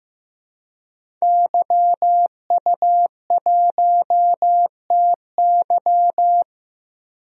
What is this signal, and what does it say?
Morse code, heard as a keyed tone, about 16 wpm, 700 Hz. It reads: YU1TY
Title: DXCC-100 Call Signs - 5 Characters - Repeated 2x - 15wpm